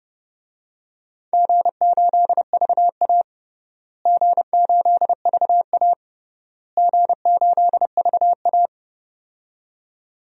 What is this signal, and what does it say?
Morse code, heard as a keyed tone, about 32 wpm, 700 Hz. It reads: G8VA G8VA G8VA
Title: DXCC-100 Call Signs - 4 Characters - Repeated 3x - 30wpm